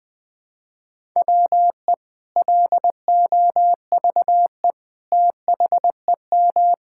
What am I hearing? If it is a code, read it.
Morse code, heard as a keyed tone, about 21 wpm, 700 Hz. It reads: WE LOVE THEM
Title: Sentences from Top 1600 Words 20wpm